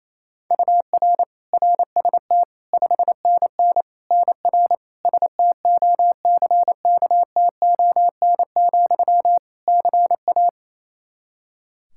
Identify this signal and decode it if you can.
Morse code, heard as a keyed tone, about 27 wpm, 700 Hz. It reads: UR RST 5NN NR STOCKTON, CA
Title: Practice QSO - Exchange 1 - Beginning - 28wpm